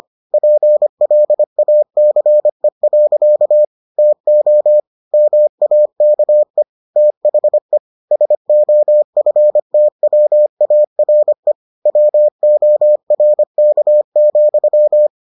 Morse code, heard as a keyed tone, about 25 wpm, 600 Hz. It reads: PLACE. TO MAKE THE SOFTWARE WORK,